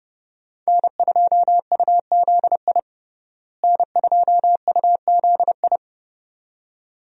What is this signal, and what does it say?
Morse code, heard as a keyed tone, about 31 wpm, 700 Hz. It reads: N2UZI N2UZI
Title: USA Call Signs - 5 Characters - Repeated 2x - 30wpm